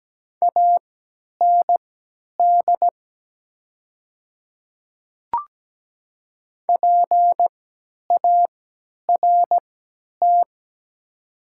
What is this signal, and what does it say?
Morse code, heard as a keyed tone, about 17 wpm, spaced as for 10 wpm, 700 Hz. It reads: AND PART
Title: 3x Character Spacing - Top 300 Words - 17wpm